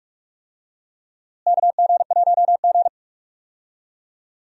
Morse code, heard as a keyed tone, about 45 wpm, 700 Hz. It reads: KG1G
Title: USA Call Signs - 4 Characters - Repeated 2x - 45wpm